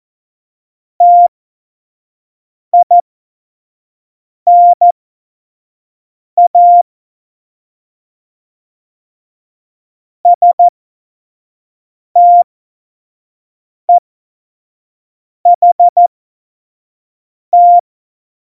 Morse code, heard as a keyed tone, about 14 wpm, spaced as for 5 wpm, 700 Hz. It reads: TINA STEHT
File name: Morsekurs / Witze / Witz 19. Februar 2019, 25 F70